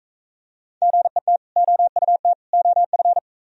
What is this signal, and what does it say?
Morse code, heard as a keyed tone, about 42 wpm, 700 Hz. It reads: GET OUT OF